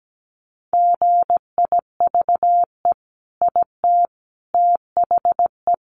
Morse code, heard as a keyed tone, about 17 wpm, 700 Hz. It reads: GIVE IT THE
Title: Sentences from Top 1300 Words 17wpm